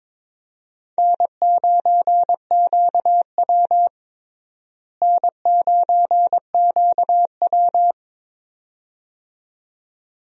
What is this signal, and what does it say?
Morse code, heard as a keyed tone, about 22 wpm, 700 Hz. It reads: N9QW N9QW
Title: USA Call Signs - 4 Characters - Repeated 2x - 22wpm